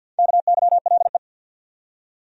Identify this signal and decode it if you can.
Morse code, heard as a keyed tone, about 52 wpm, 700 Hz. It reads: KYLE
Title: USA Common Names - Max 4 Letters - 50wpm